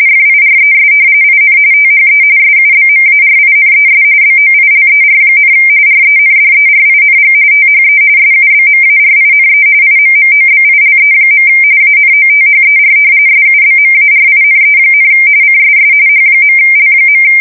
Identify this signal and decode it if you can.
RTTY, about 45.45 baud. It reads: FIRES IN THE SUBMARINES BOILER, RENDERING THE SUBMARINE IMMOBILE. ANOTHER SUBMARINE, CSS H.L.
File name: fsk170 / usnavy3